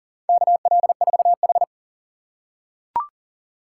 Morse code, heard as a keyed tone, about 40 wpm, 700 Hz. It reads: KL4H
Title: USA Call Signs - 4 Characters - 3x Character Spacing - 40wpm